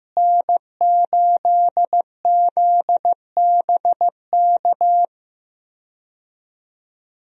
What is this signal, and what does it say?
Morse code, heard as a keyed tone, about 15 wpm, 700 Hz. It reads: N8ZBK
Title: USA Call Signs - 5 Characters - Repeated 3x - 15wpm